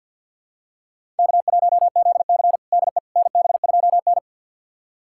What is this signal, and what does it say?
Morse code, heard as a keyed tone, about 52 wpm, 700 Hz. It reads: K1ZC DE NB2N
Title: USA Callsign DE Callsign - 4 Characters - 50wpm